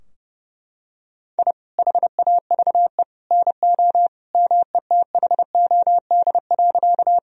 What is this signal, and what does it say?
Morse code, heard as a keyed tone, about 30 wpm, 700 Hz. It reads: I HAVE NO METHOD.
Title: Sentences from Top 1000 Words - Review Part 3 - 30wpm